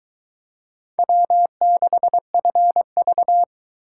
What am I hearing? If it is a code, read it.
Morse code, heard as a keyed tone, about 23 wpm, 700 Hz. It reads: W6FV